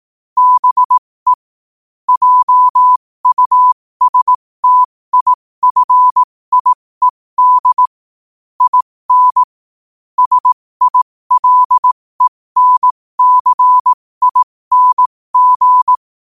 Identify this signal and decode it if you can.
Morse code, heard as a keyed tone, about 18 wpm, spaced as for 15 wpm, 1000 Hz. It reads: BE JUSTIFIED IN SILENCING
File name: qotdfiles / 15wpm